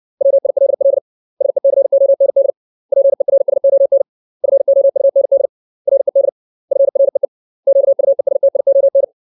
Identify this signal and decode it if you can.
Morse code, heard as a keyed tone, about 60 wpm, 550 Hz. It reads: WILL SECOND PERSON FOUND RED FREE QUESTION
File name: 632 words / 632 Words 60WPM